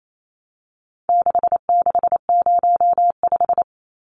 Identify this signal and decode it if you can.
Morse code, heard as a keyed tone, about 28 wpm, 700 Hz. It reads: 6605
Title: Numbers - 4-Digit 28wpm